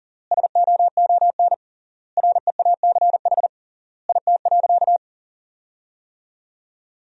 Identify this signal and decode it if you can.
Morse code, heard as a keyed tone, about 40 wpm, 700 Hz. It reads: SOON REACH IT.
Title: 3x Word-Spacing - Sentences from Top 400 Words - 40wpm